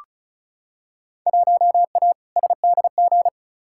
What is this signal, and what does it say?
Morse code, heard as a keyed tone, about 36 wpm, 700 Hz. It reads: 1A SDG
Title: Field Day Exchanges Encore - 35wpm